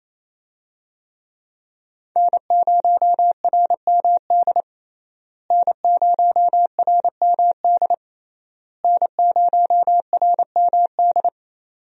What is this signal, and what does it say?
Morse code, heard as a keyed tone, about 28 wpm, 700 Hz. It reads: N0RMD N0RMD N0RMD
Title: USA Call Signs - 5 Characters - Repeated 3x - 28wpm